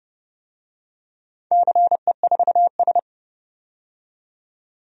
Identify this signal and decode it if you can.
Morse code, heard as a keyed tone, about 30 wpm, 700 Hz. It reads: CE4S
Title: DXCC-100 Call Signs - 4 Characters - 30wpm